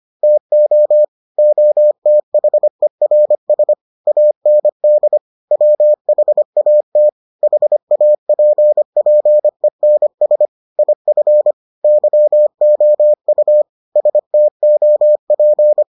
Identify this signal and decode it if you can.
Morse code, heard as a keyed tone, about 25 wpm, 600 Hz. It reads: TO OTHERS AND WHAT HAPPENS IF YOU STOP